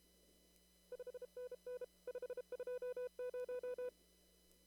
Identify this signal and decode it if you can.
Morse code, heard as a keyed tone, about 32 wpm, 500 Hz. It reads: 5NN 520
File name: CW / kt5j wpx cw-op-k5pi